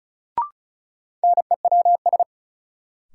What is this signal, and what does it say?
Morse code, heard as a keyed tone, about 36 wpm, 700 Hz. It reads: NEWS
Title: Sentences from Top 800 Words - New Word Drill - 35wpm